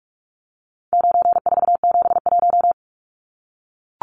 Morse code, heard as a keyed tone, about 44 wpm, 700 Hz. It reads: 9471
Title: Numbers - 4-Digit 45wpm